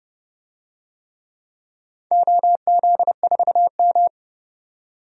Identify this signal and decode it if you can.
Morse code, heard as a keyed tone, about 31 wpm, 700 Hz. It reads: OZ4M
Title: DXCC-100 Call Signs - 4 Characters - 30wpm